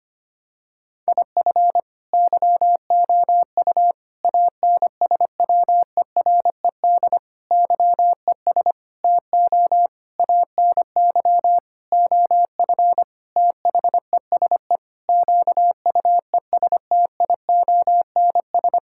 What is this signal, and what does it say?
Morse code, heard as a keyed tone, about 25 wpm, 700 Hz. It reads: IF YOU ANSWERED YES TO ANY OF THESE QUESTIONS
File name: practice / 25wpm / Sentences with 9-Word Phrases - 25wpm